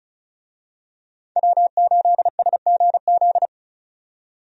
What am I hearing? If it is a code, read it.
Morse code, heard as a keyed tone, about 34 wpm, 700 Hz. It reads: W8SGZ
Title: USA Call Signs - 5 Characters - 35wpm